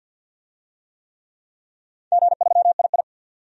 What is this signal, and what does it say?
Morse code, heard as a keyed tone, about 50 wpm, 700 Hz. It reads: G3II
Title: DXCC-100 Call Signs - 4 Characters - 50wpm